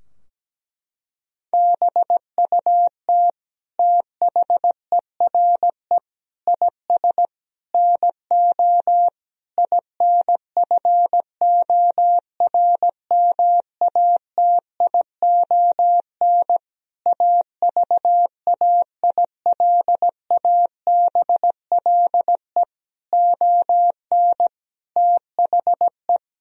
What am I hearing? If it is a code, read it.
Morse code, heard as a keyed tone, about 17 wpm, 700 Hz. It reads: BUT THERE IS NO INFORMATION AVAILABLE ON THE
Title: Sentences from Top 1000 Words - Review Part 3 - 17wpm